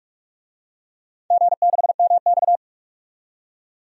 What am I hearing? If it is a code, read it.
Morse code, heard as a keyed tone, about 46 wpm, 700 Hz. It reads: G6MX
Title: DXCC-100 Call Signs - 4 Characters - 45wpm